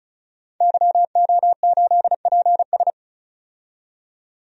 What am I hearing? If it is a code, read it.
Morse code, heard as a keyed tone, about 36 wpm, 700 Hz. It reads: YO8PS
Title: DXCC-100 Call Signs - 5 Characters - 35wpm